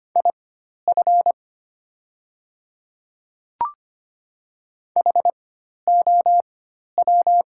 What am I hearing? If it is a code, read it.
Morse code, heard as a keyed tone, about 25 wpm, spaced as for 12 wpm, 700 Hz. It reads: IF HOW
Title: 4x Character Spacing - Top 100 Words - 25wpm